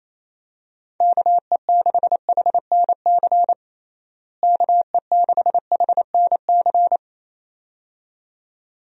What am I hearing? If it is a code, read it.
Morse code, heard as a keyed tone, about 27 wpm, 700 Hz. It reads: KE6HNC KE6HNC
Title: USA Call Signs - 6 Characters - Repeated 2x - 28wpm